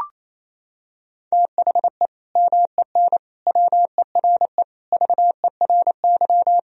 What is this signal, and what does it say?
Morse code, heard as a keyed tone, about 28 wpm, 700 Hz. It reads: THE MEN WERE VERY
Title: Sentences from Top 1000 Words - Review Part 3 - 28wpm